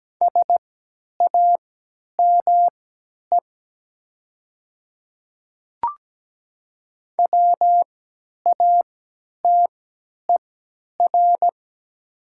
Morse code, heard as a keyed tone, about 17 wpm, spaced as for 10 wpm, 700 Hz. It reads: SAME WATER
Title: 3x Character Spacing - Top 300 Words - 17wpm